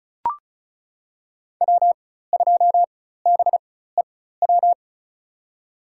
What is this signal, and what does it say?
Morse code, heard as a keyed tone, about 34 wpm, spaced as for 16 wpm, 700 Hz. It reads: W2BEW
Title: USA Call Signs - 5 Characters - 4x Character Spacing - 35wpm